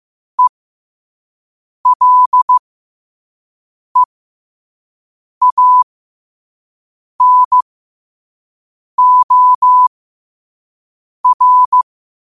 Morse code, heard as a keyed tone, about 15 wpm, spaced as for 5 wpm, 1000 Hz. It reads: ELEANOR
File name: qotdfiles / 5wpm